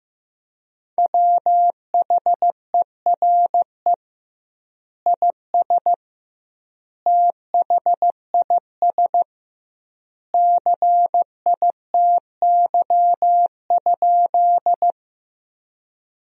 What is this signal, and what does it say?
Morse code, heard as a keyed tone, about 15 wpm, 700 Hz. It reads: WHERE IS THIS CITY?
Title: 2x Word-Spacing - Sentences from Top 300 Words - 15wpm